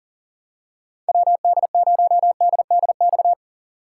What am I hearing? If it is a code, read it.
Morse code, heard as a keyed tone, about 40 wpm, 700 Hz. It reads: WD0DDX
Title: USA Call Signs - 6 Characters - 2x Character Spacing - 40wpm